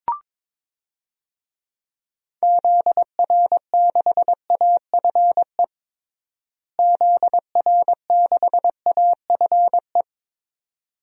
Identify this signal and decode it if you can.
Morse code, heard as a keyed tone, about 21 wpm, 700 Hz. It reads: ZR6AFE ZR6AFE
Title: DXCC-100 Call Signs - 6 Characters - Repeated 2x - 22wpm